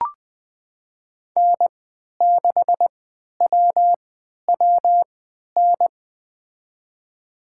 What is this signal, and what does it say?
Morse code, heard as a keyed tone, about 21 wpm, spaced as for 12 wpm, 700 Hz. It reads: N6WWN
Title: USA Call Signs - 3x Character Spacing - 20wpm